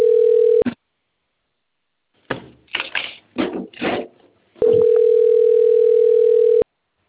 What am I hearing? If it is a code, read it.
Ringback tone (North America), 440 + 480 Hz.
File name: out- - - - - .58955